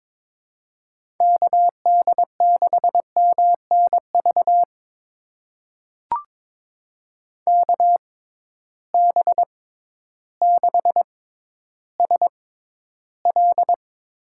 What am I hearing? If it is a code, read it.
Morse code, heard as a keyed tone, about 21 wpm, spaced as for 12 wpm, 700 Hz. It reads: KD6MNV KB6SL
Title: USA Call Signs - 6 Characters - 6x Character Spacing - 22wpm